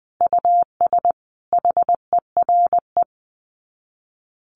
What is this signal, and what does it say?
Morse code, heard as a keyed tone, about 20 wpm, 700 Hz. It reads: US HERE
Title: Sets of 5 Words 20wpm